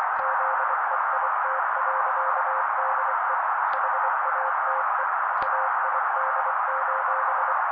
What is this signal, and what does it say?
Morse code, heard as a keyed tone, about 24 wpm, 530 Hz. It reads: ZEIT.DE HATE AID:
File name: daily / 23wpm noise